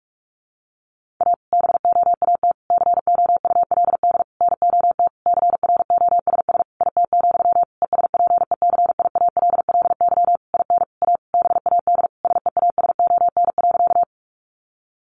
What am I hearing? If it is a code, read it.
Morse code, heard as a keyed tone, about 44 wpm, 700 Hz. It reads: A BOAT COULD NOT CROSS IT, ESPECIALLY IN A BAD SEASON.
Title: Sentences from Top 1400 Words 45wpm